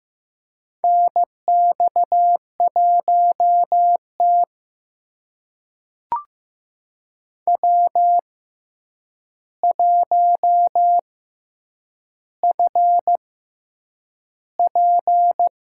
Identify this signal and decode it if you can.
Morse code, heard as a keyed tone, about 15 wpm, spaced as for 8 wpm, 700 Hz. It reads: NX1T W1FP
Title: USA Call Signs - 4 Characters - 6x Character Spacing - 15wpm